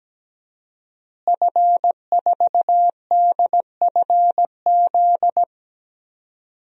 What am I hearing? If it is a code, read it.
Morse code, heard as a keyed tone, about 17 wpm, 700 Hz. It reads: F4DFZ
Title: DXCC-100 Call Signs - 5 Characters - Repeated 2x - 17wpm